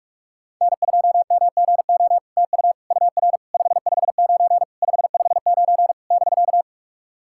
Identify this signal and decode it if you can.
Morse code, heard as a keyed tone, about 47 wpm, 700 Hz. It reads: N2MGO TU UR 559 559 <BK>
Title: SOTA QSO Practice 45wpm